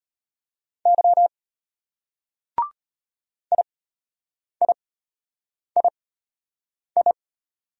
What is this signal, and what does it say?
Morse code, heard as a keyed tone, about 33 wpm, spaced as for 8 wpm, 700 Hz. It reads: Y IIII